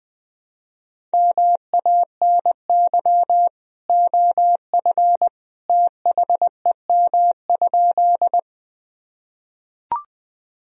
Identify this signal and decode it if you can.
Morse code, heard as a keyed tone, about 20 wpm, 700 Hz. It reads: MANY OF THEM?
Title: Sentences from Top 1000 Words - Review Part 1 - 20wpm